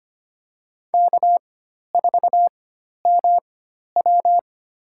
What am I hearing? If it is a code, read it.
Morse code, heard as a keyed tone, about 25 wpm, spaced as for 12 wpm, 700 Hz. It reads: K4MW
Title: USA Call Signs - 4 Characters - 4x Character Spacing - 25wpm